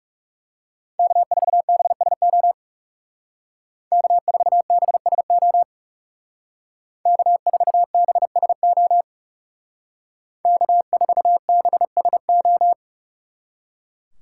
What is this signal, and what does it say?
Morse code, heard as a keyed tone, about 35 wpm, 700 Hz. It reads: K4BSO K4BSO K4BSO K4BSO
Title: Speed Racing - USA 1x3 Call Signs 30wpm